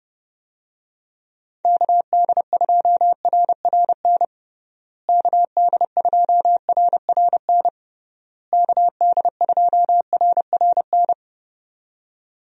Morse code, heard as a keyed tone, about 30 wpm, 700 Hz. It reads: KD2RRN KD2RRN KD2RRN
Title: USA Call Signs - 6 Characters - Repeated 3x - 30wpm